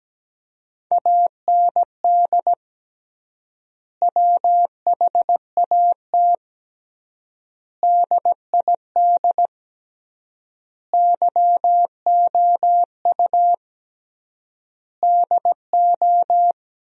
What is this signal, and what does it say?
Morse code, heard as a keyed tone, about 17 wpm, 700 Hz. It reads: AND WHAT DID YOU DO
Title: 3x Word-Spacing - Sentences from Top 100 Words - 17wpm